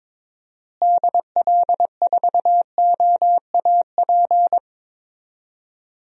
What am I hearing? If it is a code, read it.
Morse code, heard as a keyed tone, about 22 wpm, 700 Hz. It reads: DL4OAP
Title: USA Call Signs - 6 Characters - Repeated 3x - 22wpm